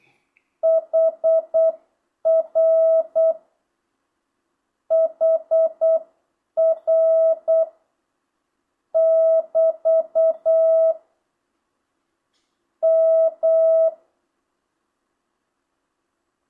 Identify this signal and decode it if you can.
Morse code, heard as a keyed tone, about 8 wpm, 650 Hz. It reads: HR HR <BT> M